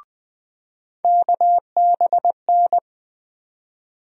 Morse code, heard as a keyed tone, about 20 wpm, 700 Hz. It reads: KBN
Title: USA Call Sign Postfixes - 3 Letters 20wpm